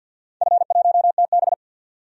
Morse code, heard as a keyed tone, about 51 wpm, 700 Hz. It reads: F1TB
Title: DXCC-100 Call Signs - 4 Characters - 50wpm